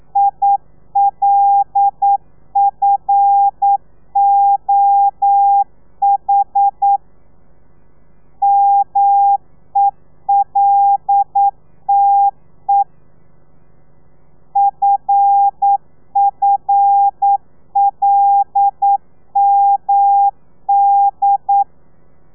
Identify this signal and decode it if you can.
Morse code, heard as a keyed tone, about 9 wpm, 800 Hz. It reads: ILFOH MELTE FFLMD
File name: telegrafia / i2tfj / Lez5 40